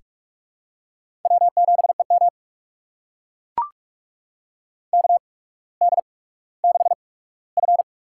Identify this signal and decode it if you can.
Morse code, heard as a keyed tone, about 44 wpm, spaced as for 19 wpm, 700 Hz. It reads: W7EM KD6F
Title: USA Call Signs - 4 Characters - 8x Character Spacing - 45wpm